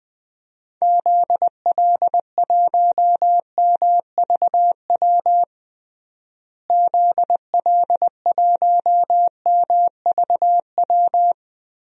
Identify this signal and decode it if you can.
Morse code, heard as a keyed tone, about 20 wpm, 700 Hz. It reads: ZL1MVW ZL1MVW
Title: DXCC-100 Call Signs - 6 Characters - Repeated 3x - 20wpm